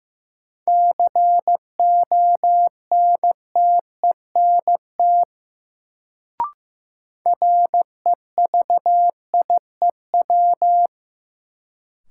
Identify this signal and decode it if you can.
Morse code, heard as a keyed tone, about 15 wpm, 700 Hz. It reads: CONTENT REVIEW
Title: Sentences from Top 900 Words - New Word Drill - 15wpm